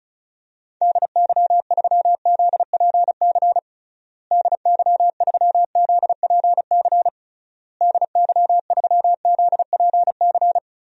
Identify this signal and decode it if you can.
Morse code, heard as a keyed tone, about 36 wpm, 700 Hz. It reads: DY3ZPC DY3ZPC DY3ZPC
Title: DXCC-100 Call Signs - 6 Characters - Repeated 3x - 35wpm